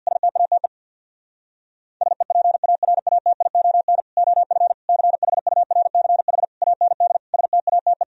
Morse code, heard as a keyed tone, about 64 wpm, 700 Hz. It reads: STATE SEPARATION OF CHURCH AND STATE